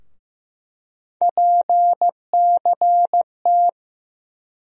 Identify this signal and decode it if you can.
Morse code, heard as a keyed tone, about 15 wpm, 700 Hz. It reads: PCT
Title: USA Call Sign Postfixes - 3 Letters 15wpm